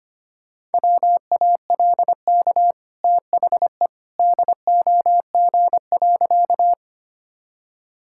Morse code, heard as a keyed tone, about 25 wpm, 700 Hz. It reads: WALK THE DOG.
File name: Easy Sentences with Hints - 25wpm